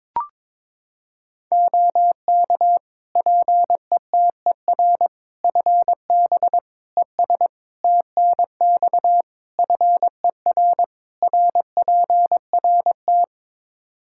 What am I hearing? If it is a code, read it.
Morse code, heard as a keyed tone, about 22 wpm, 700 Hz. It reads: OK PETER FB ES TNX FER RPRT
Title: K1USN SST QSO Practice 22wpm